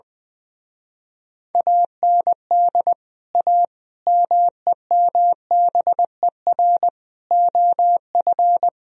Morse code, heard as a keyed tone, about 20 wpm, 700 Hz. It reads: AND A MEMBER OF
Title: Speed-Racing - Sets of 5 Words 20wpm